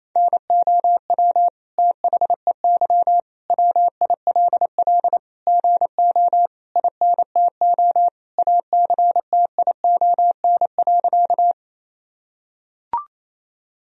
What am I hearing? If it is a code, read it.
Morse code, heard as a keyed tone, about 28 wpm, 700 Hz. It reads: NOW THEY WILL GO INTO ACTION.
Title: Sentences from Top 1000 Words - Review Part 4 - 28wpm